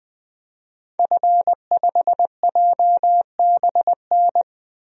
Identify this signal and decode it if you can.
Morse code, heard as a keyed tone, about 20 wpm, 700 Hz. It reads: F5JBN
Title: DXCC-100 Call Signs - 5 Characters - 20wpm